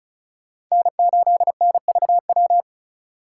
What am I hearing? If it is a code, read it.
Morse code, heard as a keyed tone, about 38 wpm, 700 Hz. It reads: N8NVW